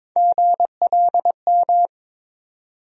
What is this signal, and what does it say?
Morse code, heard as a keyed tone, about 21 wpm, 700 Hz. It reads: GLM
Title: USA Call Sign Postfixes - 3 Letters 22wpm